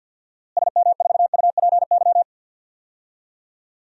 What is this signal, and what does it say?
Morse code, heard as a keyed tone, about 47 wpm, 700 Hz. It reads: SM4UPY
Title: DXCC-100 Call Signs - 6 Characters - Repeated 2x - 50wpm